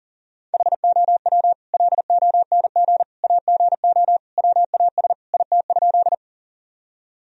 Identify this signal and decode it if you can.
Morse code, heard as a keyed tone, about 40 wpm, 700 Hz. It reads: HOW LONG AGO WAS IT?
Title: Sentences from Top 1000 Words - Review Part 3 - 40wpm